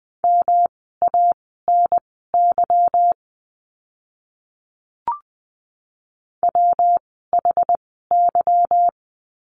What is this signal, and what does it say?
Morse code, heard as a keyed tone, about 20 wpm, spaced as for 14 wpm, 700 Hz. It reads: MANY WHY
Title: 206 Lesson 33f - Letter j - Words 20wpm